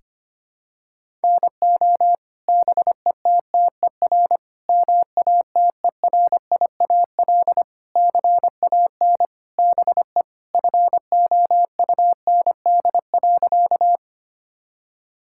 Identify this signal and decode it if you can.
Morse code, heard as a keyed tone, about 25 wpm, 700 Hz. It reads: NO BETTER MATERIAL CAN BE FOUND.
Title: Sentences from Top 1000 Words - Review Part 5 - 25wpm